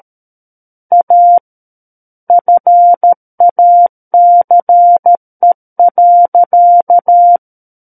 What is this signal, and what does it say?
Morse code, heard as a keyed tone, about 13 wpm, 700 Hz. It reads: A FACE.
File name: Book One / 04-Chapter 4 - 13 WPM